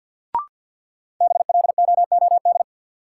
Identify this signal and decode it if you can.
Morse code, heard as a keyed tone, about 49 wpm, 700 Hz. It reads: BLOOD
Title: Sentences from Top 1000 Words - Review Part 2 - New Word Drill - 50wpm